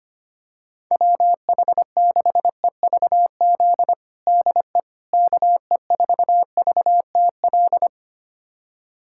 Morse code, heard as a keyed tone, about 25 wpm, 700 Hz. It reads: WH6EVZ DE KE4VTL
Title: USA Callsign DE Callsign - 6 Characters - 25wpm